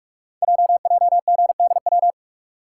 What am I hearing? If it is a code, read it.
Morse code, heard as a keyed tone, about 44 wpm, 690 Hz. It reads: JJGDW
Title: Code Groups - 5 Letters - 45wpm